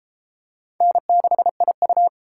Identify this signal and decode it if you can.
Morse code, heard as a keyed tone, about 29 wpm, 700 Hz. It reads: N6IU